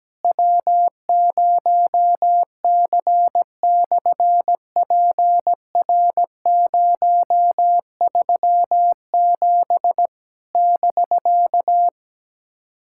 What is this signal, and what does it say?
Morse code, heard as a keyed tone, about 17 wpm, 700 Hz. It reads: W0C/PR037 <BK>
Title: SOTA QSO Practice 17wpm